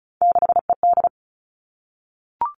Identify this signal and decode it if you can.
Morse code, heard as a keyed tone, about 34 wpm, 700 Hz. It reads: 6ED
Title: All Cut Numbers - 3-Digit 35wpm